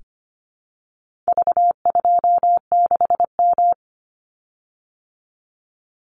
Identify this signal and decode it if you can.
Morse code, heard as a keyed tone, about 25 wpm, 700 Hz. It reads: V26M
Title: Call Signs Repeated 2x - 25wpm